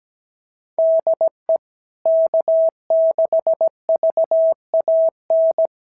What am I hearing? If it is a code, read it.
Morse code, heard as a keyed tone, about 17 wpm, 650 Hz. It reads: DE K6VAN
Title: SOTA QSO Practice 17wpm